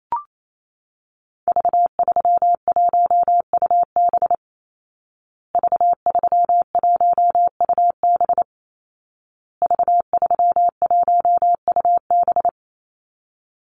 Morse code, heard as a keyed tone, about 28 wpm, 700 Hz. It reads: V31UB V31UB V31UB
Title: Call Signs Repeated 2x - 28wpm